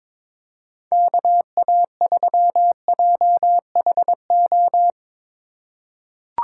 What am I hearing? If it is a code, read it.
Morse code, heard as a keyed tone, about 22 wpm, 700 Hz. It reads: KA3JHO
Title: USA Call Signs - 6 Characters - 4x Character Spacing - 22wpm